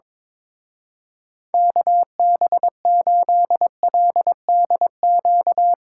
Morse code, heard as a keyed tone, about 22 wpm, 700 Hz. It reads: KB8LDQ
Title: Speed Racing - USA Call Signs 22wpm